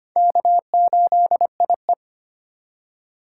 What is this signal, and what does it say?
Morse code, heard as a keyed tone, about 34 wpm, 700 Hz. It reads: K8IE